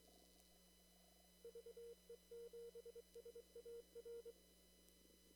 Morse code, heard as a keyed tone, about 23 wpm, 470 Hz. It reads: VE7SAR
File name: CW / kt5j wpx cw-op-k5pi